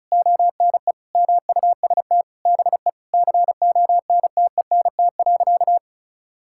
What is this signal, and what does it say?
Morse code, heard as a keyed tone, about 36 wpm, 700 Hz. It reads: ONE MUST BE CONTENT.
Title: Sentences from Top 1000 Words - Review Part 6 - 35wpm